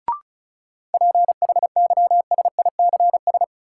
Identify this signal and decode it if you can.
Morse code, heard as a keyed tone, about 34 wpm, 700 Hz. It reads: PHYSICS
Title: Sentences from Top 1500 Words - New Word Drill Alt - 35wpm